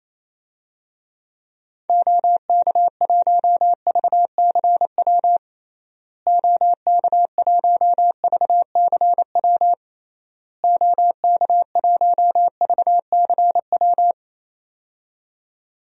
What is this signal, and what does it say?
Morse code, heard as a keyed tone, about 28 wpm, 700 Hz. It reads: OK1VCW OK1VCW OK1VCW
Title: DXCC-100 Call Signs - 6 Characters - Repeated 3x - 28wpm